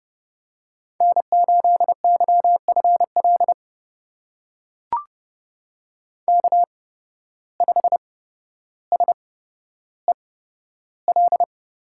Morse code, heard as a keyed tone, about 31 wpm, spaced as for 13 wpm, 700 Hz. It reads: N8YFL K5SEL
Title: USA Call Signs - 5 Characters - 8x Character Spacing - 30wpm